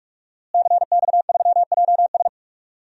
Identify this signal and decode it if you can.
Morse code, heard as a keyed tone, about 42 wpm, 700 Hz. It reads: CX3JS
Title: DXCC-100 Call Signs - 5 Characters - Repeated 2x - 45wpm